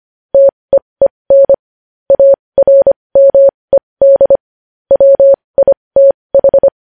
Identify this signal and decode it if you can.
Morse code, heard as a keyed tone, about 25 wpm, spaced as for 20 wpm, 550 Hz. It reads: TEEN ARMED WITH
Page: News Headlines in Morse Code at 20 WPM